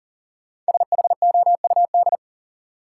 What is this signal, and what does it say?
Morse code, heard as a keyed tone, about 40 wpm, 700 Hz. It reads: SHOUD
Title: Sentences from Top 1000 Words - Review Part 4 - New Word Drill - 40wpm